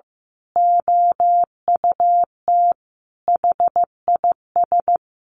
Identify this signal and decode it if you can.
Morse code, heard as a keyed tone, about 15 wpm, 700 Hz. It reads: OUT HIS
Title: A Sound Of Thunder 15wpm